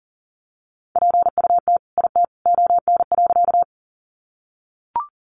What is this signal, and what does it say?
Morse code, heard as a keyed tone, about 40 wpm, 700 Hz. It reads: PUT IT ON.
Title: Sentences from Top 200 Words 40wpm